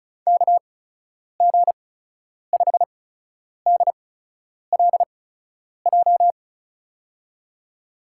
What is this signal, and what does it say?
Morse code, heard as a keyed tone, about 36 wpm, spaced as for 10 wpm, 700 Hz. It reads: KG5DLJ
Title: USA Call Signs - 6 Characters - 8x Character Spacing - 35wpm